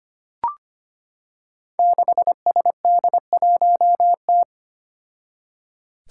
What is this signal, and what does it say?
Morse code, heard as a keyed tone, about 25 wpm, 700 Hz. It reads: 6SD1T
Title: Five Challenging Letter-Number Combinations 25wpm